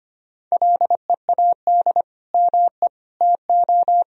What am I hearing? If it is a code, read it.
Morse code, heard as a keyed tone, about 25 wpm, 700 Hz. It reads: LEAD ME TO
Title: Sentences from Top 1000 Words - Review Part 2 - 25wpm